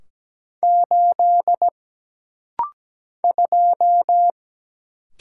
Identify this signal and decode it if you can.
Morse code, heard as a keyed tone, about 17 wpm, 700 Hz. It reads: 8 2
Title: Single Letter-Number - Rapid-Fire 17wpm